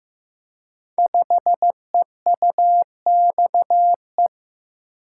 Morse code, heard as a keyed tone, about 15 wpm, 700 Hz. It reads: 5EUXE
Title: Code Groups - 5 Letter:Numbers - 15wpm